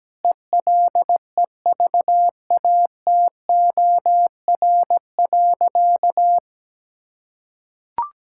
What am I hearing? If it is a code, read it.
Morse code, heard as a keyed tone, about 17 wpm, 700 Hz. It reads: ELEVATOR.
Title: Sentences from Top 1600 Words 17wpm